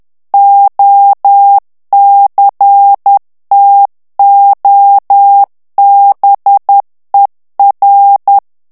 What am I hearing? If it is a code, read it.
Morse code, heard as a keyed tone, about 10 wpm, 800 Hz. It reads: OCTOBER
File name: ANGLAIS texte clair 3 vitesse 10 mots minute